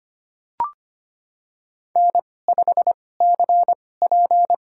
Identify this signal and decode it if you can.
Morse code, heard as a keyed tone, about 25 wpm, spaced as for 18 wpm, 700 Hz. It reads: N5CP
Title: USA Call Signs - 2x Character Spacing - 25wpm